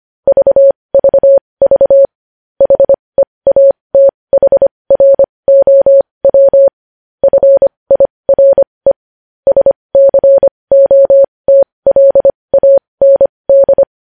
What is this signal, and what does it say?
Morse code, heard as a keyed tone, about 25 wpm, spaced as for 20 wpm, 550 Hz. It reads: VVV HEATHROW FIRE SCOTLAND
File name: podcast / News headlines-1x-25x20